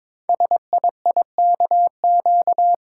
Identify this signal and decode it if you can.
Morse code, heard as a keyed tone, about 22 wpm, 700 Hz. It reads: SIIKQ
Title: Code Groups - 5 Letters - 22wpm